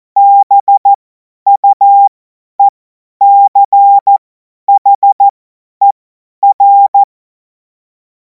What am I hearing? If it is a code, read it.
Morse code, heard as a keyed tone, about 14 wpm, spaced as for 10 wpm, 800 Hz. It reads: BUECHER
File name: Morsekurs / Witze / Witz 05. Mai 2026 50 F70